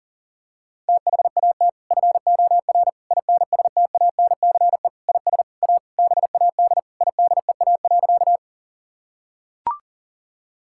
Morse code, heard as a keyed tone, about 37 wpm, 700 Hz. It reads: THAT FOR INSTANCE IS A BAD IDEA.
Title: Sentences from Top 1000 Words - Review Part 5 - 40wpm